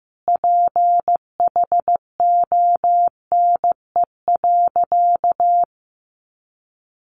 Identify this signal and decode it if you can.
Morse code, heard as a keyed tone, about 15 wpm, 700 Hz. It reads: PHONE.
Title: Sentences from Top 1100 Words 15wpm